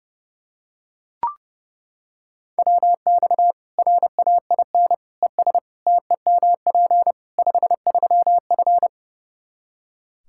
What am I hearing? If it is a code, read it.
Morse code, heard as a keyed tone, about 32 wpm, 700 Hz. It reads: WX RAIN ES TEMP 53F
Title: Practice QSO - Exchange 2 - Info - 30wpm